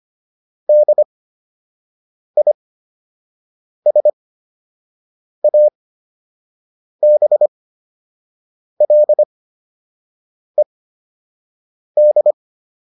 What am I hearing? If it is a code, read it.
Morse code, heard as a keyed tone, about 25 wpm, spaced as for 6 wpm, 600 Hz. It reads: DISABLED
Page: fundamental-practice-files